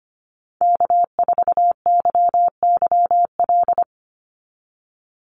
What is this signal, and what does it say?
Morse code, heard as a keyed tone, about 25 wpm, 700 Hz. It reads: K4YYL
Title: Call Signs Repeated 2x - 25wpm